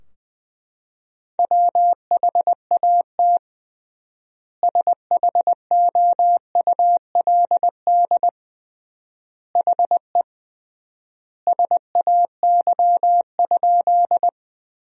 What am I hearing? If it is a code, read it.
Morse code, heard as a keyed tone, about 20 wpm, 700 Hz. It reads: WHAT SHOULD HE SAY?
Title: 3x Word-Spacing - Sentences from Top 300 Words - 20wpm